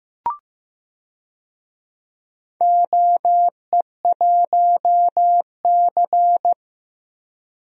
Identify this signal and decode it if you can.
Morse code, heard as a keyed tone, about 15 wpm, 700 Hz. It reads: OE1C
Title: DXCC-100 Call Signs - 4 Characters - Repeated 2x - 15wpm